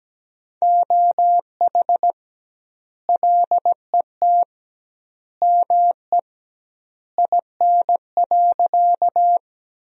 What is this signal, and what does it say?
Morse code, heard as a keyed tone, about 17 wpm, 700 Hz. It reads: OH LET ME IN.
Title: 2x Word-Spacing - Sentences from Top 500 Words - 17wpm